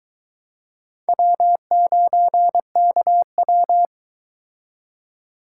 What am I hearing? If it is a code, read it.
Morse code, heard as a keyed tone, about 22 wpm, 700 Hz. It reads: W9KW